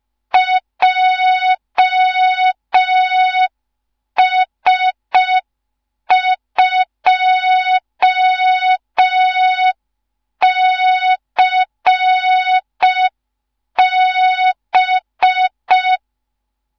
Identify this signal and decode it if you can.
Morse code, heard as a keyed tone, about 5 wpm, 3700 Hz. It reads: JS2CB
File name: Morze-HA2MN / hang / 15 GYAKORLAT